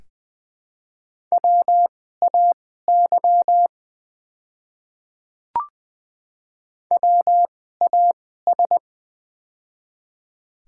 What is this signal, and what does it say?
Morse code, heard as a keyed tone, about 20 wpm, spaced as for 14 wpm, 700 Hz. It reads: WAY WAS
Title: 2x Character Spacing - Top 200 Words - 20wpm